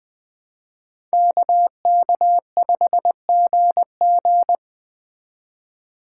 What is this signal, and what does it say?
Morse code, heard as a keyed tone, about 20 wpm, 700 Hz. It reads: KK5GG
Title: Speed Racing - USA 2x2 Call Signs 17wpm